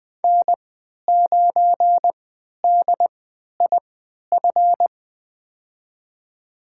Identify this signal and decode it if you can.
Morse code, heard as a keyed tone, about 20 wpm, spaced as for 11 wpm, 700 Hz. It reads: N9DIF
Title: USA Call Signs - 5 Characters - 3x Character Spacing - 20wpm